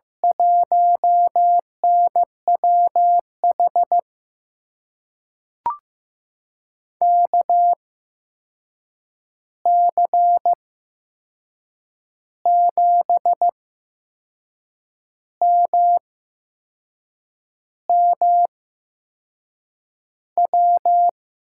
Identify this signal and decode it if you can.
Morse code, heard as a keyed tone, about 15 wpm, spaced as for 6 wpm, 700 Hz. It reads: 1NWH KC7MMW
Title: USA Call Signs - 6 Characters - 8x Character Spacing - 15wpm